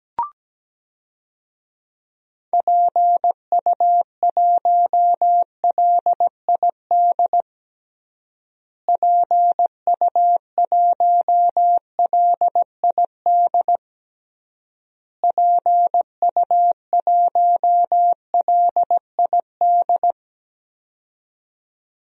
Morse code, heard as a keyed tone, about 17 wpm, 700 Hz. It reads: PU1LID PU1LID PU1LID
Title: DXCC-100 Call Signs - 6 Characters - Repeated 3x - 17wpm